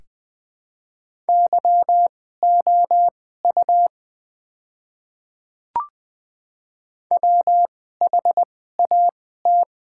Morse code, heard as a keyed tone, about 20 wpm, spaced as for 14 wpm, 700 Hz. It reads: YOU WHAT
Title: 2x Character Spacing - Top 100 Words - 20wpm